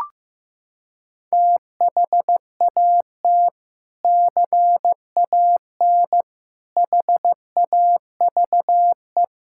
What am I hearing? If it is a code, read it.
Morse code, heard as a keyed tone, about 15 wpm, 700 Hz. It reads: THAT CAN HAVE
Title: Sentences from Top 1000 Words - Review Part 6 - 15wpm